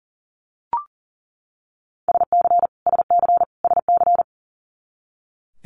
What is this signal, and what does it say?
Morse code, heard as a keyed tone, about 37 wpm, 700 Hz. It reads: SC SC SC
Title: Multi-letter Phonemes Repeated 3x 40wpm